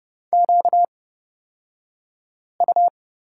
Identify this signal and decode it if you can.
Morse code, heard as a keyed tone, about 30 wpm, spaced as for 5 wpm, 700 Hz. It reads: QU
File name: Lesson 38d - Letter q - Two Characters 30wpm